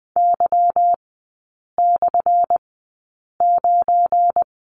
Morse code, heard as a keyed tone, about 20 wpm, spaced as for 8 wpm, 700 Hz. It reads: Y/9
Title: 218 Lesson 35e - Number 0 - Three Characters 20wpm